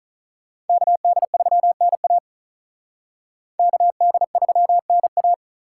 Morse code, heard as a keyed tone, about 44 wpm, 700 Hz. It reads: KD3NA KD3NA